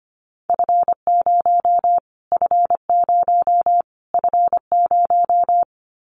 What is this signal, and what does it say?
Morse code, heard as a keyed tone, about 25 wpm, 700 Hz. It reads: F0 F0 F0
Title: Callsign Prefixes for Top 25 DXCC - Repeated 3x - 25wpm